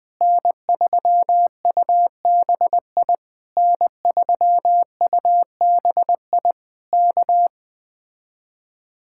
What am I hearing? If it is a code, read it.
Morse code, heard as a keyed tone, about 20 wpm, 700 Hz. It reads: N3UBI N3UBI K
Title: Speed-Racing Common QSO Phrases Encore - 17wpm